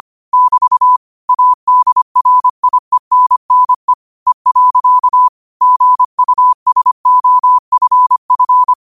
Morse code, heard as a keyed tone, about 25 wpm, 1000 Hz. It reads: <BT> ADRIENNE E. GUSOFF
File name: qotdfiles / 25wpm